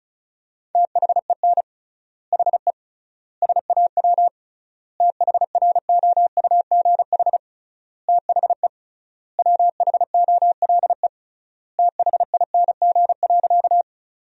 Morse code, heard as a keyed tone, about 36 wpm, 700 Hz. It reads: THEN HE SAW THROUGH THE WHOLE THING.
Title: 3x Word-Spacing - Sentences from Top 400 Words - 35wpm